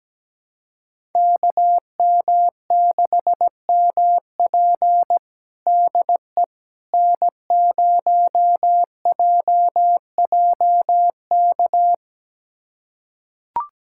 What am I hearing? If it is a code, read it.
Morse code, heard as a keyed tone, about 17 wpm, 700 Hz. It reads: KM6MP DE N0JJK
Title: USA Callsign DE Callsign - 5 Characters - 17wpm